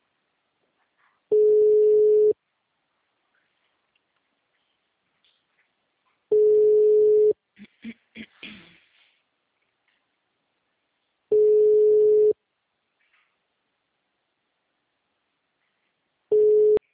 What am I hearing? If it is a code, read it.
Ringback tone (Europe, 425 Hz).
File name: out- -2051- - - .94929